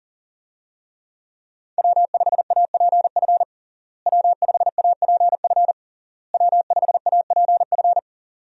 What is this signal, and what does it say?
Morse code, heard as a keyed tone, about 37 wpm, 700 Hz. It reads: W5APF W5APF W5APF
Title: USA Call Signs - 5 Characters - Repeated 3x - 40wpm